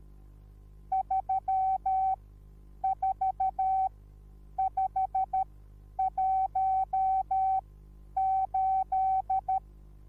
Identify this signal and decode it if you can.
Morse code, heard as a keyed tone, about 13 wpm, spaced as for 9 wpm, 760 Hz. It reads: 34518